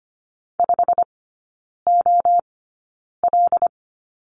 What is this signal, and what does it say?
Morse code, heard as a keyed tone, about 25 wpm, spaced as for 9 wpm, 690 Hz. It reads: 5OL
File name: Lesson 15e - Number 5 - Three Characters 25wpm